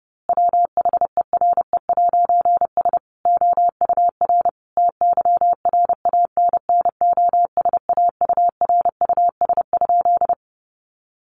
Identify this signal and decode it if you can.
Morse code, heard as a keyed tone, about 28 wpm, 700 Hz. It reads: WHERE'S OUR TYRANNOSAURUS?
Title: A Sound Of Thunder 30wpm